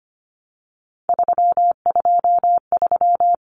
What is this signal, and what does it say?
Morse code, heard as a keyed tone, about 25 wpm, 700 Hz. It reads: 323
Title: Common Cut Numbers - 3-Digit 25wpm